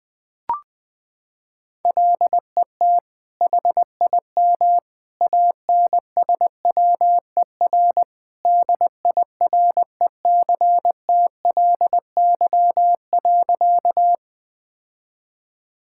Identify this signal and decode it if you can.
Morse code, heard as a keyed tone, about 20 wpm, 700 Hz. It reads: LET HIM ANSWER DIRECTLY.
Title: Sentences from Top 1000 Words - Review Part 5 - 20wpm